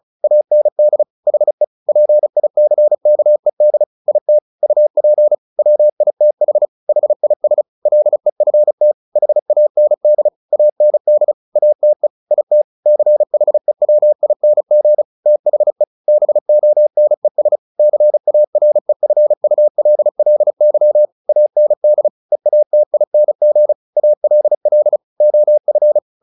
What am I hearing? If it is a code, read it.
Morse code, heard as a keyed tone, about 34 wpm, 600 Hz. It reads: AND HE PICKED IT UP WITH HIS LEFT HAND AND ATE IT CHEWING THE BONES CAREFULLY AND EATING ALL OF
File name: the old man and the sea 2 35wpm